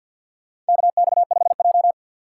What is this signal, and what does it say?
Morse code, heard as a keyed tone, about 51 wpm, 700 Hz. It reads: KX5J
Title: USA Call Signs - 4 Characters - 2x Character Spacing - 50wpm